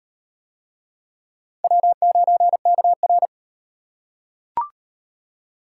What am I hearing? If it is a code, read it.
Morse code, heard as a keyed tone, about 33 wpm, 700 Hz. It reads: W9KR